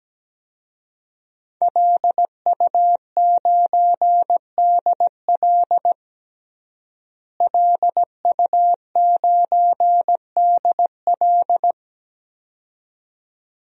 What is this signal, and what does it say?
Morse code, heard as a keyed tone, about 17 wpm, 700 Hz. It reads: LU9DL LU9DL
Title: DXCC-100 Call Signs - 5 Characters - Repeated 2x - 17wpm